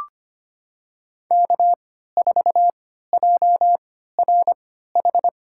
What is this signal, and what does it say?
Morse code, heard as a keyed tone, about 25 wpm, spaced as for 14 wpm, 700 Hz. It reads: K4JRH
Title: USA Call Signs - 5 Characters - 3x Character Spacing - 25wpm